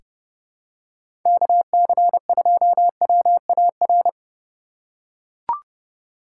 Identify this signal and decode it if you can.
Morse code, heard as a keyed tone, about 30 wpm, 700 Hz. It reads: KC2WAR
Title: USA Call Signs - 6 Characters - 3x Character Spacing - 30wpm